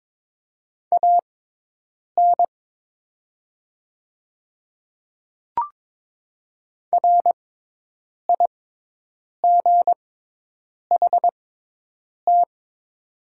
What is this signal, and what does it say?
Morse code, heard as a keyed tone, about 22 wpm, spaced as for 8 wpm, 700 Hz. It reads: AN RIGHT
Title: 6x Character Spacing - Top 300 Words - 22wpm